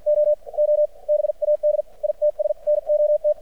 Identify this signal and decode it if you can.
Morse code, heard as a keyed tone, about 53 wpm, spaced as for 41 wpm, 600 Hz. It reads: OO BAD ITS NOT
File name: archived / CRSnet